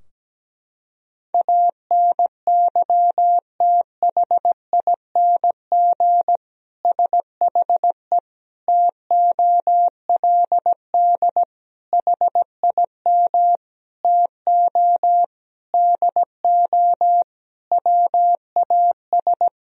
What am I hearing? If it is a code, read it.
Morse code, heard as a keyed tone, about 17 wpm, 700 Hz. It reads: ANYTHING SHE TOLD HIM TO DO WAS
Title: Sentences from Top 1000 Words - Review Part 3 - 17wpm